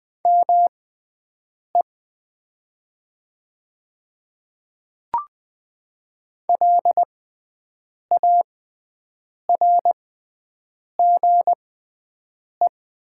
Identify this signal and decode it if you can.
Morse code, heard as a keyed tone, about 20 wpm, spaced as for 7 wpm, 700 Hz. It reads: ME LARGE
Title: 6x Character Spacing - Top 500 Words - 20wpm